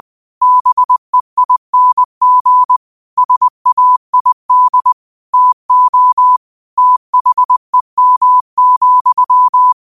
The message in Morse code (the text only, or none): BEING SAID TO THEM,